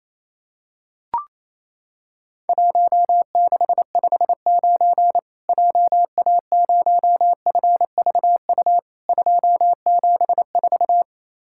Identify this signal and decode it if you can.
Morse code, heard as a keyed tone, about 28 wpm, 700 Hz. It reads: 1659 JA0FVU 274
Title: Worked All Europe DX Contest - QTC Exchange - 28wpm